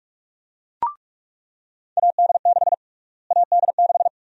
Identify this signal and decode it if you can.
Morse code, heard as a keyed tone, about 42 wpm, 700 Hz. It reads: AD6 AD6
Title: USA Call Sign Prefixes - Repeated 2x - 45wpm